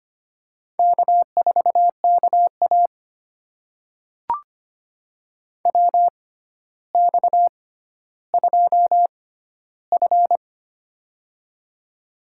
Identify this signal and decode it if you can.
Morse code, heard as a keyed tone, about 26 wpm, spaced as for 13 wpm, 700 Hz. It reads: K4KA WX2F
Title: USA Call Signs - 4 Characters - 6x Character Spacing - 25wpm